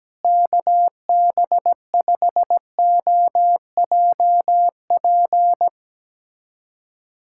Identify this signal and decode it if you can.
Morse code, heard as a keyed tone, about 17 wpm, 700 Hz. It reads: KB5OJP
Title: USA Callsign DE Callsign 17wpm